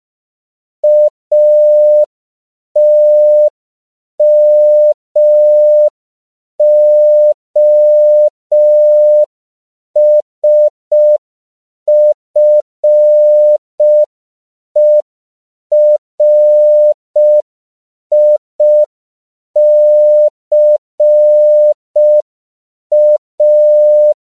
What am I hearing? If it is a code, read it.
Morse code, heard as a keyed tone, about 5 wpm, 600 Hz. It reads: ATMOSFERICA
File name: CW / 5PPM